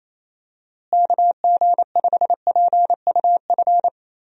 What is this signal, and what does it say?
Morse code, heard as a keyed tone, about 28 wpm, 700 Hz. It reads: KG5PUF
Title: USA Call Signs - 6 Characters - Repeated 2x - 28wpm